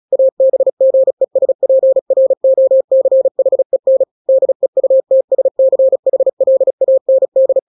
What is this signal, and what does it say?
Morse code, heard as a keyed tone, about 36 wpm, 530 Hz. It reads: ABGESPROCHEN DEUTSCHLAND
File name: daily / 35wpm